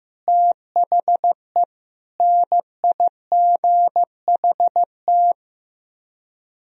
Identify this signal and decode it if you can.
Morse code, heard as a keyed tone, about 15 wpm, 700 Hz. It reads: THE NIGHT
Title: Speed-Racing - Sets of 4 Words 15wpm